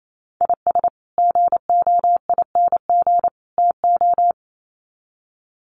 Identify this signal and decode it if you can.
Morse code, heard as a keyed tone, about 28 wpm, 700 Hz. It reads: IS GOING TO
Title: Sets of 3 Words 28wpm